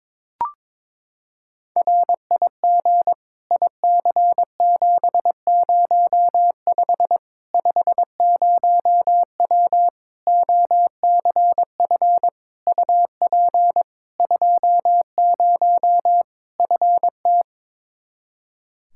Morse code, heard as a keyed tone, about 22 wpm, 700 Hz. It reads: RIG IC705 50W OCF UP 20 FT
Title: Practice QSO - Exchange 2 - Info - 22wpm